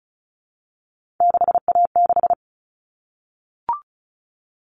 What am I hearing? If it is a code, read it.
Morse code, heard as a keyed tone, about 34 wpm, 700 Hz. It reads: 6A6
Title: All Cut Numbers - 3-Digit 35wpm